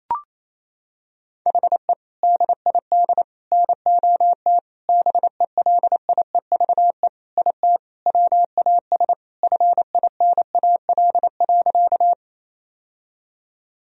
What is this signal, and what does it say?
Morse code, heard as a keyed tone, about 28 wpm, 700 Hz. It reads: HE DID NOT BELIEVE IT WAS FINAL.
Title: Sentences from Top 1000 Words - Review Part 2 - 28wpm